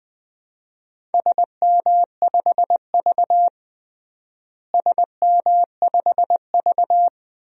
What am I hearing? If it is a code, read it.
Morse code, heard as a keyed tone, about 20 wpm, 700 Hz. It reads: SM5V SM5V
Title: DXCC-100 Call Signs - 4 Characters - Repeated 2x - 20wpm